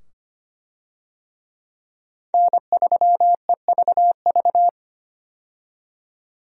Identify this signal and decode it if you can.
Morse code, heard as a keyed tone, about 25 wpm, 700 Hz. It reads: N3EVV
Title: USA Call Signs - 5 Characters - 25wpm